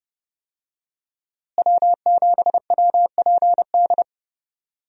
Morse code, heard as a keyed tone, about 30 wpm, 700 Hz. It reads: W7WPD
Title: USA Call Signs - 5 Characters - Repeated 2x - 30wpm